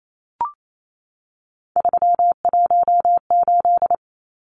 Morse code, heard as a keyed tone, about 28 wpm, 700 Hz. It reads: 318
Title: Numbers - 3-Digit 28wpm